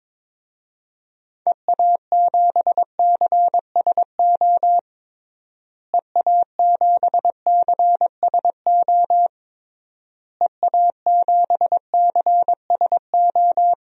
Morse code, heard as a keyed tone, about 22 wpm, 700 Hz. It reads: EA7CSO EA7CSO EA7CSO
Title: DXCC-100 Call Signs - 6 Characters - Repeated 3x - 22wpm